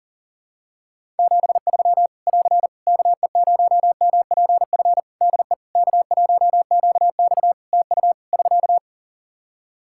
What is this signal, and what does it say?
Morse code, heard as a keyed tone, about 40 wpm, 700 Hz. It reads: 73 <AR> KE0MPF DE K1QX TU <SK>
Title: Practice QSO - Exchange 4 - Ending - 40wpm